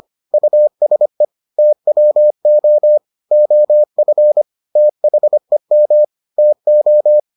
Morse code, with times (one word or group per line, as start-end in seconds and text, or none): 0.3-1.2 USE
1.6-3.0 TWO
3.3-4.4 OF
4.8-6.0 THEM
6.4-7.2 TO